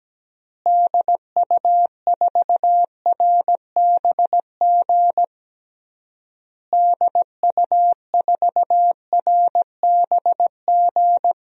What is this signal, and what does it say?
Morse code, heard as a keyed tone, about 17 wpm, 700 Hz. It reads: DU4RBG DU4RBG
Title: DXCC-100 Call Signs - 6 Characters - Repeated 3x - 17wpm